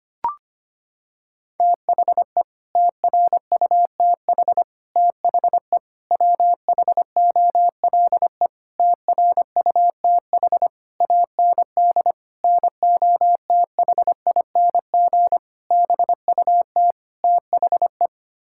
Morse code, heard as a keyed tone, about 25 wpm, 700 Hz. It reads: THE TRUTH THE WHOLE TRUTH AND NOTHING BUT THE
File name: Sets of 9 Words - 25wpm